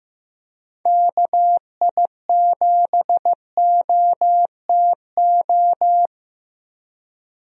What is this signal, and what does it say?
Morse code, heard as a keyed tone, about 15 wpm, 700 Hz. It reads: KI7OTO
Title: USA Call Signs - 6 Characters - 15wpm